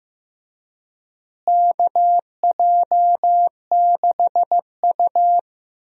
Morse code, heard as a keyed tone, about 15 wpm, 700 Hz. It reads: KJ6U
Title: USA Call Signs - 4 Characters - Repeated 3x - 15wpm